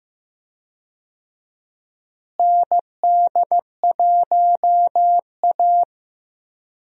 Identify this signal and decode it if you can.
Morse code, heard as a keyed tone, about 15 wpm, 700 Hz. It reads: ND1A
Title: USA Call Signs - 4 Characters - Repeated 3x - 15wpm